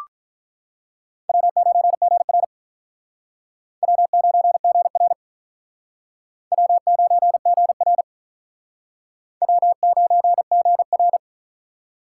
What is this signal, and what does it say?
Morse code, heard as a keyed tone, about 44 wpm, 700 Hz. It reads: W9GR W9GR W9GR W9GR
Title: Speed Racing - USA 1x2 Call Signs 35wpm